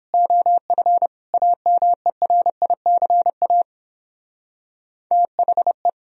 Morse code, heard as a keyed tone, about 29 wpm, 700 Hz. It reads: OF AMERICA THE